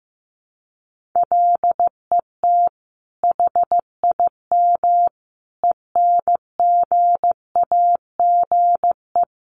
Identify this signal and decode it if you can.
Morse code, heard as a keyed tone, about 15 wpm, 700 Hz. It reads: LET HIM ENGAGE
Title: Sentences from Top 1500 Words 15wpm